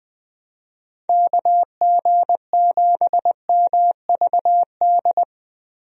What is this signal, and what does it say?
Morse code, heard as a keyed tone, about 20 wpm, 700 Hz. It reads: KG7MVD
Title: USA Call Signs - 6 Characters - Repeated 2x - 20wpm